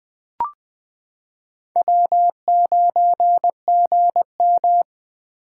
Morse code, heard as a keyed tone, about 21 wpm, 700 Hz. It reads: W9GM
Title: 2023 POTA Hunter Call Signs - 20wpm